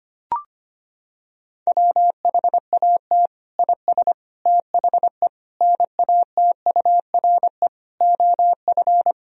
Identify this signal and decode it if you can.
Morse code, heard as a keyed tone, about 25 wpm, 700 Hz. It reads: WHAT IS THE NATURE OF
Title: Sentences from Top 1000 Words - Review Part 4 - 25wpm